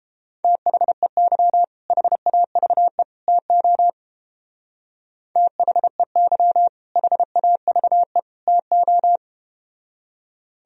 Morse code, heard as a keyed tone, about 31 wpm, 700 Hz. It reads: THEY HAVE TO THEY HAVE TO